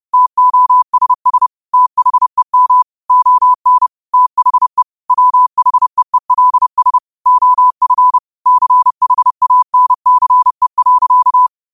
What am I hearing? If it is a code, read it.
Morse code, heard as a keyed tone, about 30 wpm, 1000 Hz. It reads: TOSS THEM ON THE WHEELS OF CHANCE.
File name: qotdfiles / 30wpm